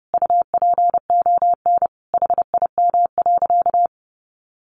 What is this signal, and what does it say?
Morse code, heard as a keyed tone, about 30 wpm, 700 Hz. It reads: UPON HIM.
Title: Sentences from Top 1200 Words 30wpm